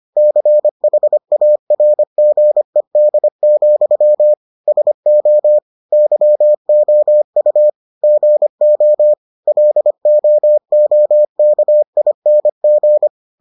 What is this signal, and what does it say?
Morse code, heard as a keyed tone, about 25 wpm, 600 Hz. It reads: CHARGED, SO YOU GO LOOKING